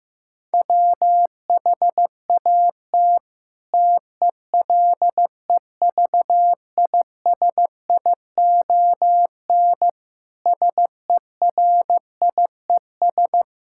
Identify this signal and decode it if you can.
Morse code, heard as a keyed tone, about 15 wpm, 700 Hz. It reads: WHAT TELEVISION SERIES
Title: Sentences from Top 1000 Words - Review Part 5 - 15wpm